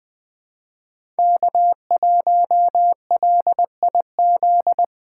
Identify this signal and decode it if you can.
Morse code, heard as a keyed tone, about 20 wpm, 700 Hz. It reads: K1LIZ
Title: 2023 SOTA Chaser Call Signs - 20wpm